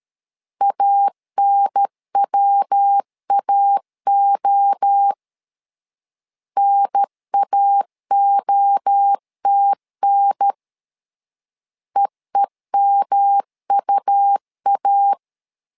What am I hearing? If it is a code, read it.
Morse code, heard as a keyed tone, about 13 wpm, 800 Hz. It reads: ANWAO NAOTN EEMUA
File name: LEZIONI / 3 MWRO / 3 MWRO 12wpm 1730